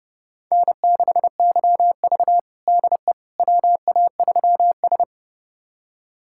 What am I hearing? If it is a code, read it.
Morse code, heard as a keyed tone, about 30 wpm, 700 Hz. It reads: N6YV DE WA3S
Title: USA Callsign DE Callsign - 4 Characters - 30wpm